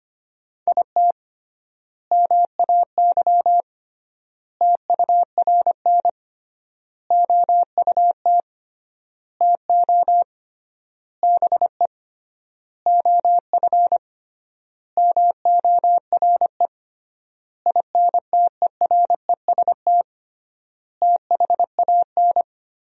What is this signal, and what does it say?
Morse code, heard as a keyed tone, about 25 wpm, 700 Hz. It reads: IT MAY TURN OUT TO BE OF MORE INTEREST THAN
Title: 3x Word-Spacing - Sentences from Top 400 Words - 25wpm